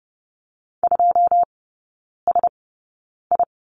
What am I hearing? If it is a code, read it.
Morse code, heard as a keyed tone, about 30 wpm, spaced as for 9 wpm, 700 Hz. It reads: 2SI
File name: Lesson 17e - Letter u - Three Characters 30wpm